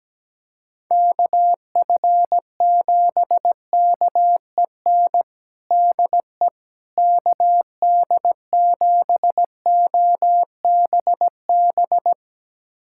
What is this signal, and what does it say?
Morse code, heard as a keyed tone, about 17 wpm, 700 Hz. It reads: KF7KEN DE KD7OBB
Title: USA Callsign DE Callsign - 6 Characters - 17wpm